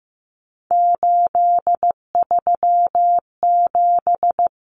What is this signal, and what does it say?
Morse code, heard as a keyed tone, about 15 wpm, 700 Hz. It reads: 837
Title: Numbers - 3-Digit 15wpm